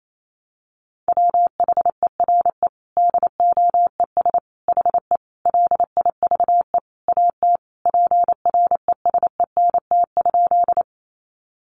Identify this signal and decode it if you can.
Morse code, heard as a keyed tone, about 28 wpm, 700 Hz. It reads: WHERE DOES HE LIVE AT PRESENT?
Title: Sentences from Top 800 Words 28wpm